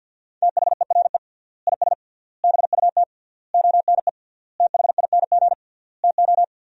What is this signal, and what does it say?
Morse code, heard as a keyed tone, about 50 wpm, 700 Hz. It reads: THERE IS BUT ONE THING TO
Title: 3x Word-Spacing - Sentences from Top 100 Words - 50wpm